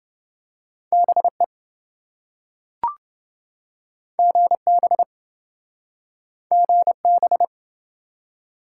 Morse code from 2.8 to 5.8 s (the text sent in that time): GB